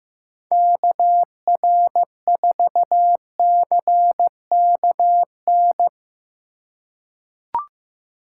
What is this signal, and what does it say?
Morse code, heard as a keyed tone, about 15 wpm, 700 Hz. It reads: KR4CKN
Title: USA Call Signs - 6 Characters - 4x Character Spacing - 15wpm